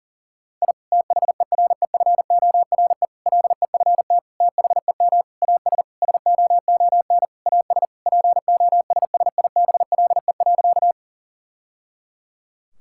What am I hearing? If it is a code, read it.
Morse code, heard as a keyed tone, about 37 wpm, 700 Hz. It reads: I THEREFORE LEFT THEM AS SOON AS POSSIBLE.
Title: Sentences from Top 1000 Words - Review Part 5 - 40wpm